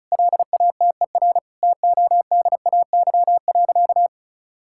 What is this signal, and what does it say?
Morse code, heard as a keyed tone, about 36 wpm, 700 Hz. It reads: LATER TODAY.
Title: Sentences from Top 1000 Words - Review Part 6 - 35wpm